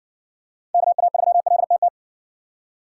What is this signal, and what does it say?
Morse code, heard as a keyed tone, about 60 wpm, 700 Hz. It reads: KA3LTT